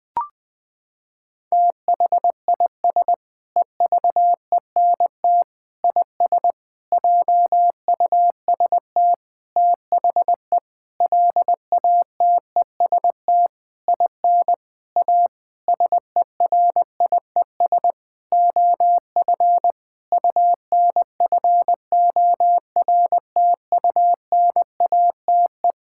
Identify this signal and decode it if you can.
Morse code, heard as a keyed tone, about 21 wpm, 700 Hz. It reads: THIS EVENT IS JUST THE LATEST IN A SERIES OF UNFORTUNATE
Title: Sentences with 8-Word Phrases - 20wpm